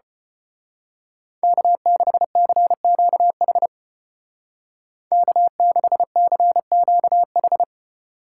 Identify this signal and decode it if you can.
Morse code, heard as a keyed tone, about 32 wpm, 700 Hz. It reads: K6CQH K6CQH